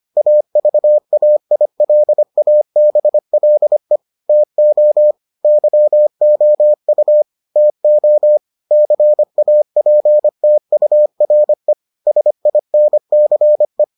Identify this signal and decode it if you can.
Morse code, heard as a keyed tone, about 25 wpm, 600 Hz. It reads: AVAILABLE TO YOU TO CAPTURE SINCE